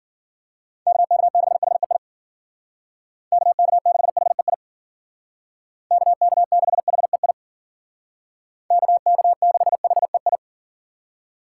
Morse code, heard as a keyed tone, about 52 wpm, 700 Hz. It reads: KK6HEI KK6HEI KK6HEI KK6HEI